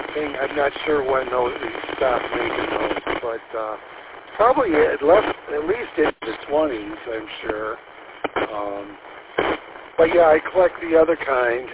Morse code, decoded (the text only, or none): none